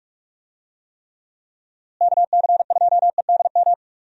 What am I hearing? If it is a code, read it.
Morse code, heard as a keyed tone, about 45 wpm, 700 Hz. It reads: KC2EDM
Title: USA Call Signs - 6 Characters - Repeated 2x - 45wpm